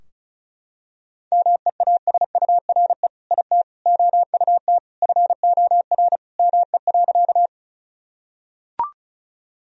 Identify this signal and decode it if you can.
Morse code, heard as a keyed tone, about 36 wpm, 700 Hz. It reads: MEASURE IT OUT FOR ME.
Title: Sentences from Top 1000 Words - Review Part 2 - 35wpm